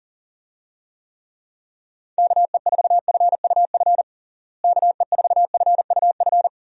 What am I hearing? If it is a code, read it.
Morse code, heard as a keyed tone, about 40 wpm, 700 Hz. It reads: KE4FUF KE4FUF
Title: USA Call Signs - 6 Characters - Repeated 2x - 40wpm